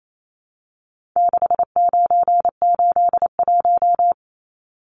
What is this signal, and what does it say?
Morse code, heard as a keyed tone, about 28 wpm, 700 Hz. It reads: 6981
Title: Numbers - 4-Digit 28wpm